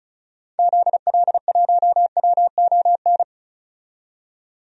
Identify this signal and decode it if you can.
Morse code, heard as a keyed tone, about 34 wpm, 700 Hz. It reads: ZL1WON
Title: DXCC-100 Call Signs - 6 Characters - 35wpm